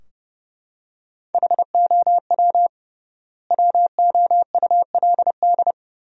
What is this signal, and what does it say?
Morse code, heard as a keyed tone, about 30 wpm, 700 Hz. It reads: HOW WOULD
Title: 3x Word-Spacing - Sentences from Top 300 Words - 30wpm